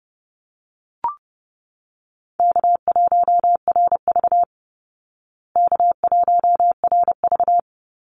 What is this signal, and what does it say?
Morse code, heard as a keyed tone, about 30 wpm, 700 Hz. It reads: K1RV K1RV
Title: Call Signs Repeated 3x - 30wpm